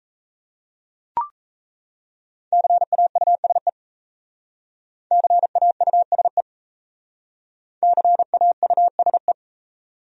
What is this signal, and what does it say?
Morse code, heard as a keyed tone, about 35 wpm, 700 Hz. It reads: CAUSE CAUSE CAUSE
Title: Speed-Racing Top 500 Words 28wpm